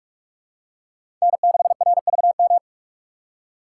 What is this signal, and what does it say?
Morse code, heard as a keyed tone, about 46 wpm, 700 Hz. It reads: N6RVM
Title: USA Call Signs - 5 Characters - 3x Character Spacing - 45wpm